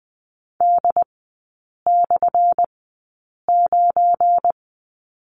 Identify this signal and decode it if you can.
Morse code, heard as a keyed tone, about 20 wpm, spaced as for 8 wpm, 700 Hz. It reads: D/9
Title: 191 Lesson 31e - Letter v - Three Characters 20wpm